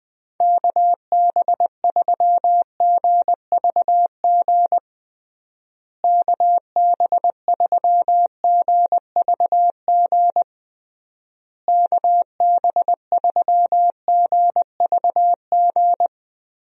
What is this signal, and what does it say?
Morse code, heard as a keyed tone, about 20 wpm, 700 Hz. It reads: KB3GVG KB3GVG KB3GVG
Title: USA Call Signs - 6 Characters - Repeated 3x - 20wpm